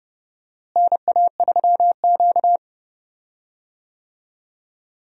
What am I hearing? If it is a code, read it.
Morse code, heard as a keyed tone, about 30 wpm, 700 Hz. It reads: NA3Q
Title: USA Call Signs - 4 Characters - Repeated 2x - 30wpm